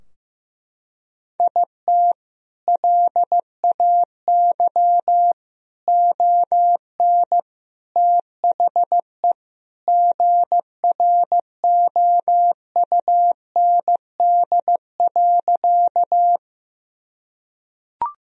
Morse code, heard as a keyed tone, about 15 wpm, 700 Hz. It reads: IT LAY ON THE GROUND.
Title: Sentences from Top 1000 Words - Review Part 2 - 15wpm